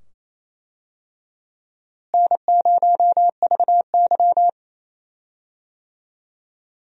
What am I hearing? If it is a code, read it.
Morse code, heard as a keyed tone, about 28 wpm, 700 Hz. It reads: N0VY
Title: USA Call Signs - 4 Characters - Repeated 2x - 28wpm